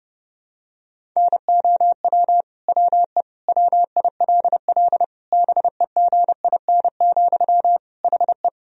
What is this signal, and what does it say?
Morse code, heard as a keyed tone, about 30 wpm, 700 Hz. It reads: NOW WE WILL BEGIN, HE
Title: Sentences from Top 1000 Words - Review Part 1 - 30wpm